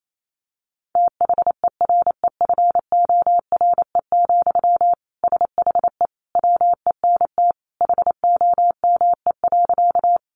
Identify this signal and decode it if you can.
Morse code, heard as a keyed tone, about 28 wpm, 700 Hz. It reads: THEREFORE, SHE WENT HOME.
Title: Sentences from Top 800 Words 28wpm